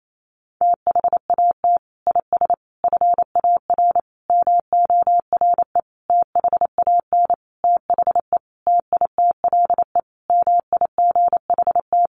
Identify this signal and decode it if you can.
Morse code, heard as a keyed tone, about 28 wpm, 700 Hz. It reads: THAT IS FAR MORE THAN THE TITLE MIGHT
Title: Sentences from Top 1200 Words 28wpm